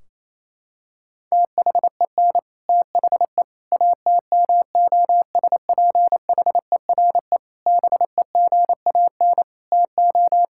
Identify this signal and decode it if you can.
Morse code, heard as a keyed tone, about 28 wpm, 700 Hz. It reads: THEN THE ATMOSPHERE BEGAN TO
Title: Sentences from Top 1600 Words 28wpm